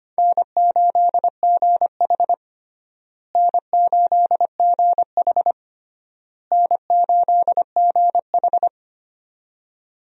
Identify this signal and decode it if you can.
Morse code, heard as a keyed tone, about 25 wpm, 700 Hz. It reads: N8GH N8GH N8GH
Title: USA Call Signs - 4 Characters - Repeated 3x - 25wpm